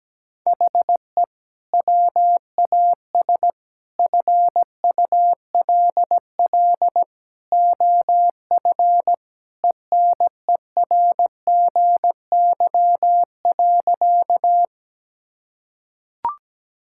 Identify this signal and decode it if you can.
Morse code, heard as a keyed tone, about 17 wpm, 700 Hz. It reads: HE WAS FULL OF ENERGY.
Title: Sentences from Top 1000 Words - Review Part 6 - 17wpm